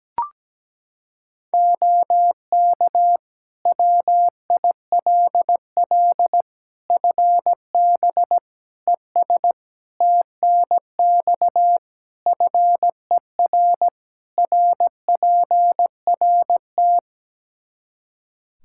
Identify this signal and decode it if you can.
Morse code, heard as a keyed tone, about 17 wpm, 700 Hz. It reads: OK WILL FB ES TNX FER RPRT
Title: Practice QSO - Exchange 2 - Info - 17wpm